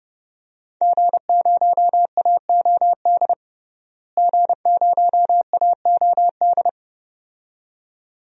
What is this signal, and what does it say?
Morse code, heard as a keyed tone, about 32 wpm, 700 Hz. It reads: G0AOD G0AOD
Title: DXCC-100 Call Signs - 5 Characters - Repeated 2x - 30wpm